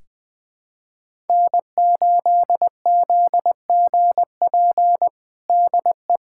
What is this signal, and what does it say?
Morse code, heard as a keyed tone, about 20 wpm, 700 Hz. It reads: N8ZGP DE
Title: USA Callsign DE Callsign - 5 Characters - 20wpm